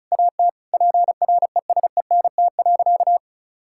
Morse code, heard as a keyed tone, about 34 wpm, 700 Hz. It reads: AT PRESENT.
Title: Sentences from Top 1000 Words - Review Part 5 - 35wpm